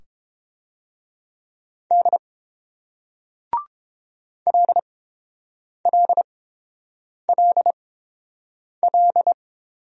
Morse code, heard as a keyed tone, about 30 wpm, spaced as for 7 wpm, 700 Hz. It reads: D LLLL